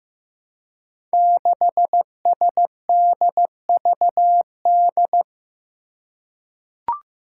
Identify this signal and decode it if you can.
Morse code, heard as a keyed tone, about 15 wpm, 700 Hz. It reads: 6SDVD
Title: Five Challenging Letter-Number Combinations 15wpm